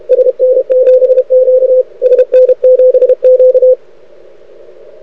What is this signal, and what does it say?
Morse code, heard as a keyed tone, about 32 wpm, 500 Hz. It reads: SN7Q SN7Q
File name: rec contest / arrl cw 2018 / test 32